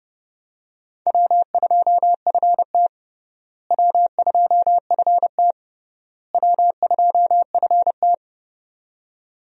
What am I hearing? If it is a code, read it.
Morse code, heard as a keyed tone, about 29 wpm, 700 Hz. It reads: W2FT W2FT W2FT
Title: USA Call Signs - 4 Characters - Repeated 3x - 30wpm